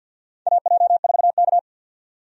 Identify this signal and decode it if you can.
Morse code, heard as a keyed tone, about 51 wpm, 700 Hz. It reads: AJ4K
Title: USA Call Signs - 4 Characters - 50wpm